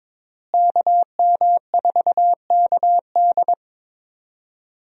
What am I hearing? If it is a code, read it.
Morse code, heard as a keyed tone, about 22 wpm, 700 Hz. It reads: KM4KD
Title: USA Call Signs - 5 Characters - Repeated 3x - 22wpm